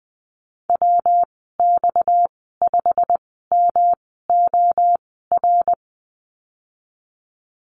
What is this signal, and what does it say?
Morse code, heard as a keyed tone, about 20 wpm, spaced as for 15 wpm, 700 Hz. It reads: WX5MOR
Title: 231 Lesson 37g - Letter x - Call Signs 20wpm